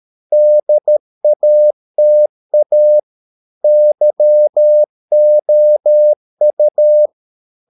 Morse code, heard as a keyed tone, about 13 wpm, 600 Hz. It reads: DATA YOU